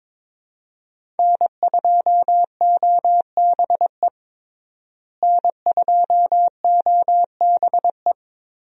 Morse code, heard as a keyed tone, about 22 wpm, 700 Hz. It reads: N2OBE N2OBE
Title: USA Call Signs - 5 Characters - Repeated 3x - 22wpm